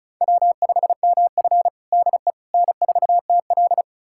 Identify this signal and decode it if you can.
Morse code, heard as a keyed tone, about 36 wpm, 700 Hz. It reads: W5MF DE N4TL
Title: USA Callsign DE Callsign - 4 Characters - 35wpm